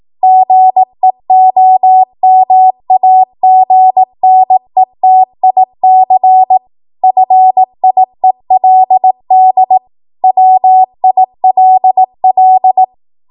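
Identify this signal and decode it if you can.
Morse code, heard as a keyed tone, about 18 wpm, 750 Hz. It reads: GEOMAGNETIC FIELD WILL
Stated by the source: 18 WPM Morse code practice from a national amateur radio association's code archive.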